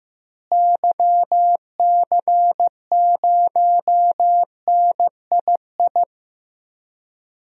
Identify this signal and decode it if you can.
Morse code, heard as a keyed tone, about 15 wpm, 700 Hz. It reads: YC0NII
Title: DXCC-100 Call Signs - 6 Characters - Repeated 2x - 15wpm